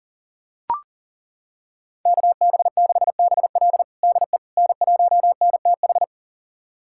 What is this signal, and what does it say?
Morse code, heard as a keyed tone, about 40 wpm, 690 Hz. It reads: KB6BL DE N1NTH
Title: USA Callsign DE Callsign - 5 Characters - 40wpm